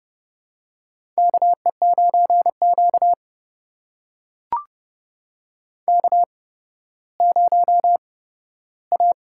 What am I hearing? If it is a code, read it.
Morse code, heard as a keyed tone, about 31 wpm, spaced as for 15 wpm, 700 Hz. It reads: KE9Q K0A
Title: USA Call Signs - 8x Character Spacing - 30wpm